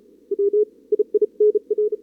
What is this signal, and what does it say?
Morse code, heard as a keyed tone, about 36 wpm, 410 Hz. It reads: W II NR
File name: CWSprint / Feb2022 / k5tr-feb-na-sprint-cw-k5ot -0000